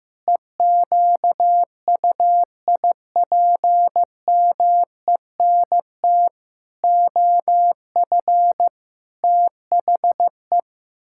Morse code, heard as a keyed tone, about 16 wpm, 700 Hz. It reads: EQUIPMENT OF THE
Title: Pangram Challenge v2 - 15wpm